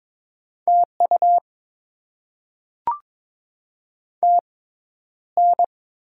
Morse code, heard as a keyed tone, about 22 wpm, spaced as for 11 wpm, 700 Hz. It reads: TU TN
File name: Top 90 QSO Elements - 6x Character Spacing - 22wpm